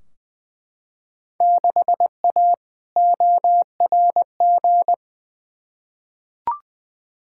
Morse code, heard as a keyed tone, about 20 wpm, 700 Hz. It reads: 6A ORG
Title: Field Day Exchanges Encore - 20wpm